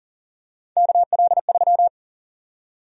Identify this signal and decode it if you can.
Morse code, heard as a keyed tone, about 40 wpm, 700 Hz. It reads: KL3
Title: USA Call Sign Prefixes - Repeated 2x - 40wpm